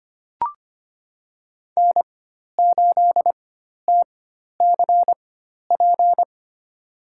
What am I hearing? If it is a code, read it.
Morse code, heard as a keyed tone, about 25 wpm, spaced as for 12 wpm, 700 Hz. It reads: N8TCP
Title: USA Call Signs - 5 Characters - 4x Character Spacing - 25wpm